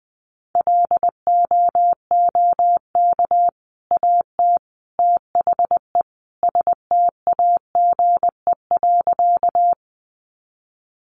Morse code, heard as a keyed tone, about 20 wpm, 700 Hz. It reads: LOOK AT THE STAGE.
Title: Sentences from Top 1200 Words 20wpm